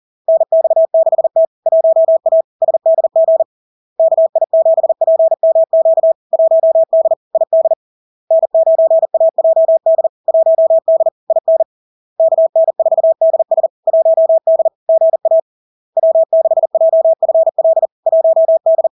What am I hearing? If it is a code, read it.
Morse code, heard as a keyed tone, about 40 wpm, 650 Hz. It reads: NX6T 1A SDG KI7PMQ 1D ID N9AJD 1D IN KN4DS 1D GA W6JFL 1D
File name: LARK-FD-2022-Log-in-CW-40wpm-650-Hz